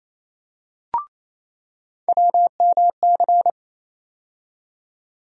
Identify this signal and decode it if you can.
Morse code, heard as a keyed tone, about 28 wpm, 700 Hz. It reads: WMC
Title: USA Call Sign Postfixes - 3 Letters 28wpm